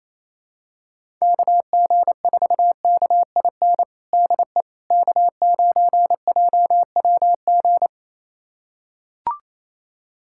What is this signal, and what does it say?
Morse code, heard as a keyed tone, about 28 wpm, 700 Hz. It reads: KG4KIN DE K9JWG
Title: USA Callsign DE Callsign 28wpm